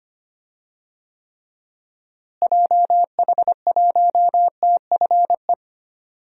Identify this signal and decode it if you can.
Morse code, heard as a keyed tone, about 25 wpm, 700 Hz. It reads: JH1TFE
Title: DXCC-100 Call Signs - 6 Characters - Repeated 2x - 25wpm